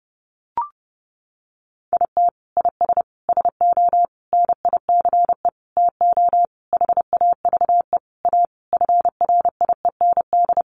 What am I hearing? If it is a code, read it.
Morse code, heard as a keyed tone, about 30 wpm, 700 Hz. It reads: IT IS SO NICE TO HAVE A FRIEND
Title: Sentences from Top 1100 Words 30wpm